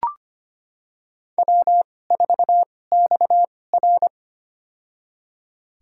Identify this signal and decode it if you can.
Morse code, heard as a keyed tone, about 25 wpm, spaced as for 18 wpm, 700 Hz. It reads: W4XR
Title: USA Call Signs - 4 Characters - 2x Character Spacing - 25wpm